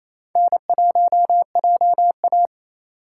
Morse code, heard as a keyed tone, about 28 wpm, 700 Hz. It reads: N1JA
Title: USA Call Signs - 4 Characters - 4x Character Spacing - 28wpm